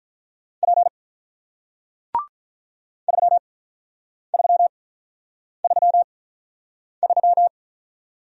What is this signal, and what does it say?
Morse code, heard as a keyed tone, about 46 wpm, spaced as for 9 wpm, 700 Hz. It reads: P 3333